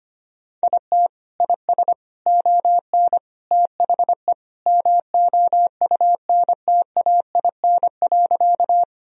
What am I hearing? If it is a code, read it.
Morse code, heard as a keyed tone, about 25 wpm, 700 Hz. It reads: IT IS ON THE MOUNTAIN.
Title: Sentences from Top 1000 Words - Review Part 1 - 25wpm